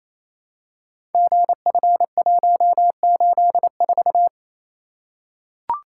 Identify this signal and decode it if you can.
Morse code, heard as a keyed tone, about 27 wpm, 700 Hz. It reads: GF184
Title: Code Groups - 5 Letter:Numbers - 28wpm